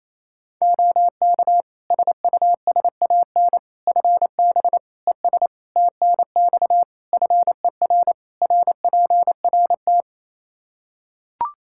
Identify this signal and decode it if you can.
Morse code, heard as a keyed tone, about 27 wpm, 700 Hz. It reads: OK SUSAN FB ES TNX FER RPRT
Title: Practice QSO - Exchange 2 - Info - 28wpm